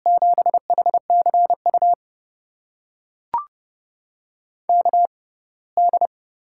Morse code, heard as a keyed tone, about 32 wpm, spaced as for 21 wpm, 700 Hz. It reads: 7HCU KD
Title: USA Call Signs - 6 Characters - 6x Character Spacing - 30wpm